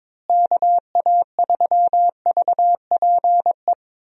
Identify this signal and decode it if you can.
Morse code, heard as a keyed tone, about 22 wpm, 700 Hz. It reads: KA3VPE
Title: USA Call Signs - 6 Characters - Repeated 3x - 22wpm